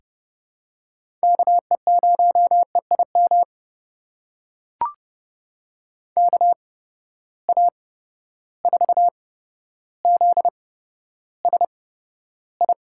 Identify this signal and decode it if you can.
Morse code, heard as a keyed tone, about 30 wpm, spaced as for 13 wpm, 700 Hz. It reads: KE0EIM KA4ZSI
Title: USA Call Signs - 6 Characters - 8x Character Spacing - 30wpm